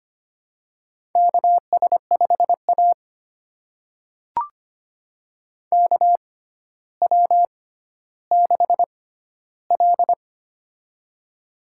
Morse code, heard as a keyed tone, about 25 wpm, spaced as for 13 wpm, 700 Hz. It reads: KS5A KW6L
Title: USA Call Signs - 4 Characters - 6x Character Spacing - 25wpm